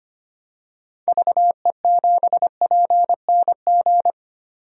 Morse code, heard as a keyed tone, about 25 wpm, 700 Hz. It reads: VE7PNG
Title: DXCC-100 Call Signs - 6 Characters - 25wpm